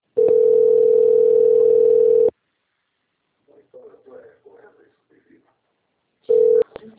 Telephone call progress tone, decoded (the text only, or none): ringback tone